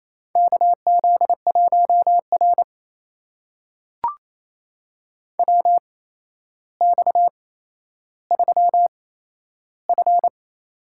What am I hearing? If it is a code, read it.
Morse code, heard as a keyed tone, about 27 wpm, spaced as for 12 wpm, 700 Hz. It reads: KZ1R WX3F
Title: USA Call Signs - 4 Characters - 8x Character Spacing - 28wpm